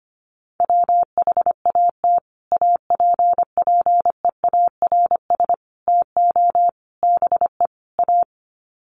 Morse code, heard as a keyed tone, about 25 wpm, 700 Hz. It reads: WHAT APPEARS TO BE A
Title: Sets of 5 Words Encore2 25wpm